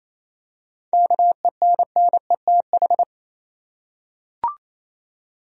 Morse code, heard as a keyed tone, about 28 wpm, 700 Hz. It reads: KENNETH
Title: USA Common Names - 28wpm